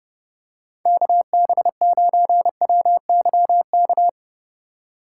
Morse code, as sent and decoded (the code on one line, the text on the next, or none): -.- -... ----. .-- -.-- -.-
KB9WYK